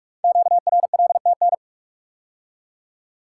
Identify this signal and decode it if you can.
Morse code, heard as a keyed tone, about 44 wpm, 700 Hz. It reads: QRLTN
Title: Code Groups - 5 Letters - 45wpm